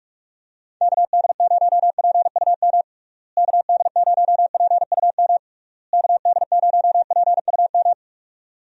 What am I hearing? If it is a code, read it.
Morse code, heard as a keyed tone, about 41 wpm, 700 Hz. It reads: KD0PUM KD0PUM KD0PUM
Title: USA Call Signs - 6 Characters - Repeated 3x - 45wpm